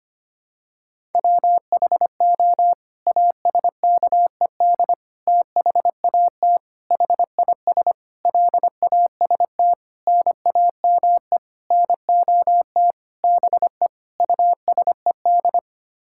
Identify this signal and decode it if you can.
Morse code, heard as a keyed tone, about 25 wpm, 700 Hz. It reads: WHO ASKED THAT HIS LAST NAME NOT BE USED
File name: Sets of 9 Words - 25wpm